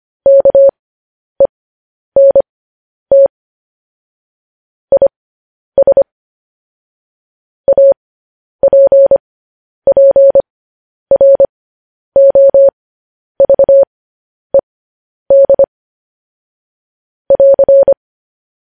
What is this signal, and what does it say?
Morse code, heard as a keyed tone, about 25 wpm, spaced as for 10 wpm, 550 Hz. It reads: KENT IS APPROVED <AR>
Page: news-headlines-in-morse-code-at-10-wpm